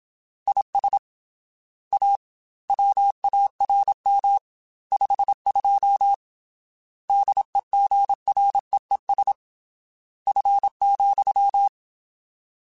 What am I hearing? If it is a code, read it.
Morse code, heard as a keyed tone, about 27 wpm, 800 Hz. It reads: IS A WARM 52 DEGREES F,
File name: Weather Sets / Weather Sets-mp3s / Weather 4 10 @25